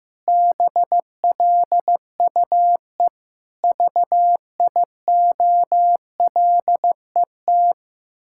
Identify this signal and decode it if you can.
Morse code, heard as a keyed tone, about 15 wpm, 700 Hz. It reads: BLUE VIOLET
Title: Pangram Challenge v2 - 15wpm